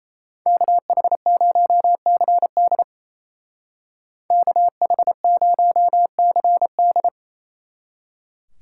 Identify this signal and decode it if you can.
Morse code, heard as a keyed tone, about 36 wpm, 700 Hz. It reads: KH0CD KH0CD